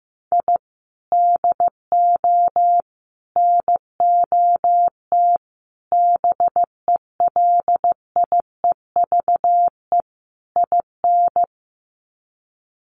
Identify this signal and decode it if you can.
Morse code, heard as a keyed tone, about 15 wpm, 700 Hz. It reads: I DO NOT BELIEVE IN
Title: Sets of 5 Words Encore 15wpm